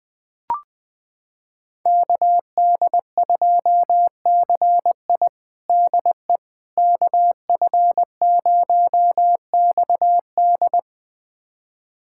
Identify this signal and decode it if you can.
Morse code, heard as a keyed tone, about 20 wpm, 700 Hz. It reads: KD2CI DE KF0XD
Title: USA Callsign DE Callsign - 5 Characters - 20wpm